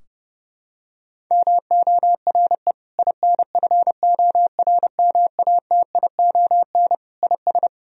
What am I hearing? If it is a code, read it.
Morse code, heard as a keyed tone, about 30 wpm, 700 Hz. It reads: MORE INFORMATION IS
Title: Sentences from Top 1000 Words - Review Part 3 - 30wpm